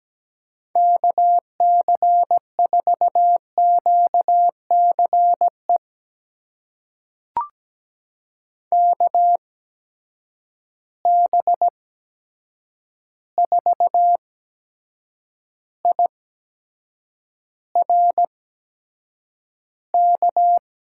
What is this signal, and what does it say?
Morse code, heard as a keyed tone, about 17 wpm, spaced as for 7 wpm, 700 Hz. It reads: KC4QCE KB4IRK
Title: USA Call Signs - 6 Characters - 8x Character Spacing - 17wpm